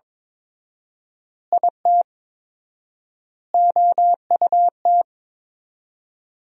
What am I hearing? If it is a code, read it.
Morse code, heard as a keyed tone, about 22 wpm, 700 Hz. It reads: IT OUT
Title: 4x Word-Spacing - Sentences from Top 200 Words - 22wpm